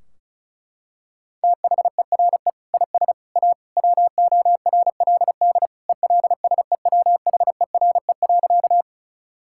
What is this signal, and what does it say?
Morse code, heard as a keyed tone, about 34 wpm, 700 Hz. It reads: THERE IS A WORLD ELSEWHERE.
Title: Sentences from Top 1600 Words 35wpm